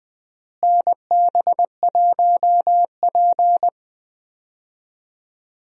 Morse code, heard as a keyed tone, about 21 wpm, 700 Hz. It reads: NB1P
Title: USA Call Signs - 4 Characters - Repeated 3x - 20wpm